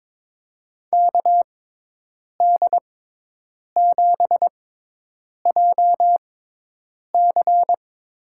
Morse code, heard as a keyed tone, about 22 wpm, spaced as for 8 wpm, 700 Hz. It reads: KD7JC
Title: USA Call Signs - 6x Character Spacing - 22wpm